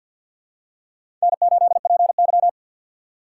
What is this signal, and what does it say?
Morse code, heard as a keyed tone, about 48 wpm, 700 Hz. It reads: N8PY
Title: USA Call Signs - 4 Characters - 2x Character Spacing - 50wpm